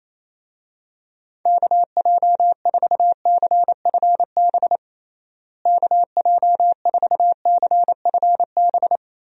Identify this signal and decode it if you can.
Morse code, heard as a keyed tone, about 27 wpm, 700 Hz. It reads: KJ4CFB KJ4CFB
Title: USA Call Signs - 6 Characters - Repeated 2x - 28wpm